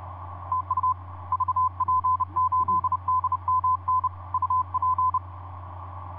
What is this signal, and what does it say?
Morse code, heard as a keyed tone, about 30 wpm, 1020 Hz, with radio noise. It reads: TU VP8DMN UP
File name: vp8dmn ja1ghh etc 30m cw 23jan2010 2351